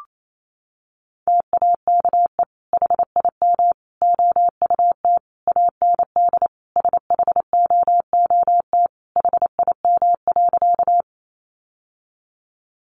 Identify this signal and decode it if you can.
Morse code, heard as a keyed tone, about 28 wpm, 700 Hz. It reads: TAKE HIM OUT AND SHOOT HIM.
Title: Sentences from Top 1200 Words 28wpm